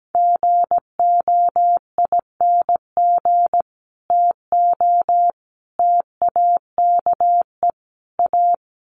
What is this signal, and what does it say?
Morse code, heard as a keyed tone, about 17 wpm, 700 Hz. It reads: GOING TO TAKE A
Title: Sets of 4 Words 17wpm